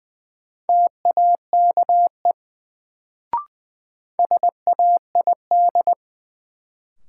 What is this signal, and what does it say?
Morse code, heard as a keyed tone, about 20 wpm, 700 Hz. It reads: TAKE SAID
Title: Sets of 2 Words - New Word Drill - 20wpm